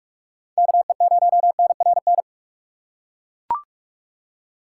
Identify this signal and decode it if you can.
Morse code, heard as a keyed tone, about 42 wpm, 700 Hz. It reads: KE0NRN
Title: USA Call Signs - 6 Characters - 4x Character Spacing - 45wpm